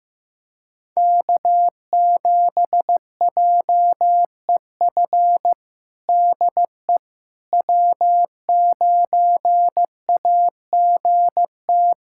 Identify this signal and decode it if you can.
Morse code, heard as a keyed tone, about 16 wpm, 700 Hz. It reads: K7JEF DE W9AGT
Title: USA Callsign DE Callsign - 5 Characters - 15wpm